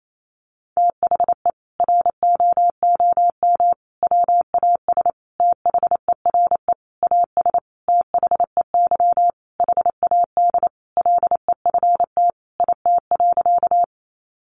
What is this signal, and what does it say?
Morse code, heard as a keyed tone, about 28 wpm, 700 Hz. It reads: THE ROOM WAS THERE AS THEY HAD LEFT IT.
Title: A Sound Of Thunder 28wpm